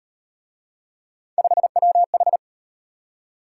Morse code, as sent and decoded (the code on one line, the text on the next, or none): ..... .-- ....
5WH